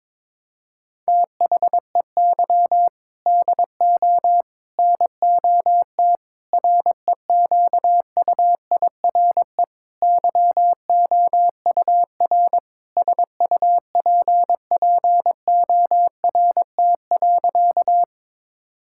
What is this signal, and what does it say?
Morse code, heard as a keyed tone, about 22 wpm, 700 Hz. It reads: THEY DO NOT REQUIRE YOUR SUPPORT.
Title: Sentences from Top 1000 Words - Review Part 6 - 22wpm